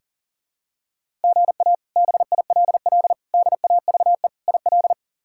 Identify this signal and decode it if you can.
Morse code, heard as a keyed tone, about 37 wpm, 700 Hz. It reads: GA BILL DAVE IL
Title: K1USN SST QSO Practice - 40wpm